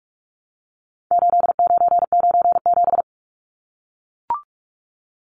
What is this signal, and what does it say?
Morse code, heard as a keyed tone, about 44 wpm, 700 Hz. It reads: 8997
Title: Numbers - 4-Digit 45wpm